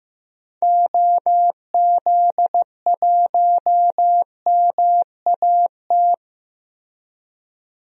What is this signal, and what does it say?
Morse code, heard as a keyed tone, about 15 wpm, 700 Hz. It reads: OZ1MAT
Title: DXCC-100 Call Signs - 6 Characters - 15wpm